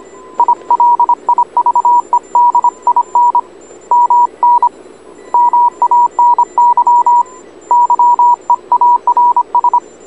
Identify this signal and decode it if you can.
Morse code, heard as a keyed tone, about 26 wpm, 960 Hz. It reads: I LIVED IN MN MANY YEARS